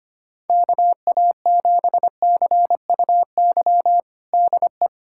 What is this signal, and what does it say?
Morse code, heard as a keyed tone, about 25 wpm, 700 Hz. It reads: KA7CUY DE
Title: USA Callsign DE Callsign 25wpm